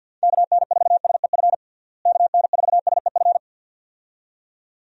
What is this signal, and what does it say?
Morse code, heard as a keyed tone, about 52 wpm, 700 Hz. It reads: KN4SEF KN4SEF
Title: USA Call Signs - 6 Characters - Repeated 2x - 50wpm